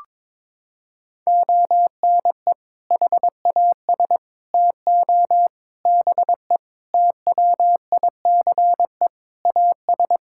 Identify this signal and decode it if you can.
Morse code, heard as a keyed tone, about 22 wpm, 700 Hz. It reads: ONE HAS TO BE TWICE AS
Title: Sentences from Top 1600 Words 22wpm